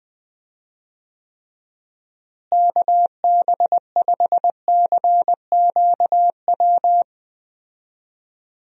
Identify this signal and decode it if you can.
Morse code, heard as a keyed tone, about 20 wpm, 700 Hz. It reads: KB5CQW
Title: USA Call Signs - 6 Characters - Repeated 2x - 20wpm